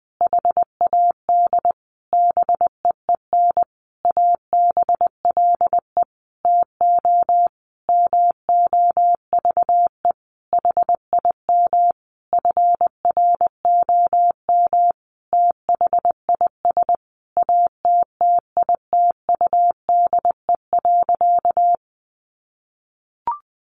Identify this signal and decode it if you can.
Morse code, heard as a keyed tone, about 20 wpm, 700 Hz. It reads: HAD BEEN ABLE TO MOVE HIM FROM THIS ATTITUDE.
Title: Sentences from Top 1300 Words 20wpm